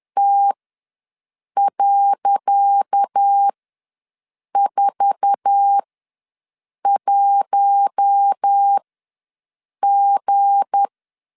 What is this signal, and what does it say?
Morse code, heard as a keyed tone, about 10 wpm, spaced as for 6 wpm, 800 Hz. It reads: T.41G
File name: LEZIONI / 10 SIGNS / 10C DOT ws 10wpm 4213